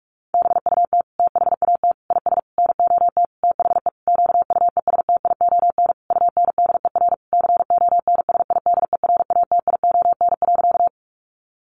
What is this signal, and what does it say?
Morse code, heard as a keyed tone, about 44 wpm, 700 Hz. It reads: BUT THAT IS NOT THE QUESTION UNDER CONSIDERATION.
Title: Sentences from Top 1500 Words 45wpm